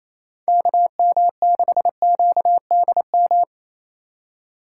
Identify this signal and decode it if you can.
Morse code, heard as a keyed tone, about 28 wpm, 700 Hz. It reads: KM6QDM
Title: USA Call Signs - 2x Character Spacing - 28wpm